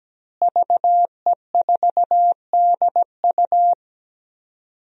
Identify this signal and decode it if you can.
Morse code, heard as a keyed tone, about 17 wpm, 700 Hz. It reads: VE4DU
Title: DXCC-100 Call Signs - 5 Characters - Repeated 2x - 17wpm